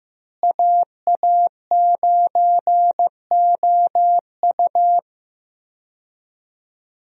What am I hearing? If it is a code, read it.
Morse code, heard as a keyed tone, about 15 wpm, 700 Hz. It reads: AA9OU
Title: USA Call Signs - 5 Characters - Repeated 3x - 15wpm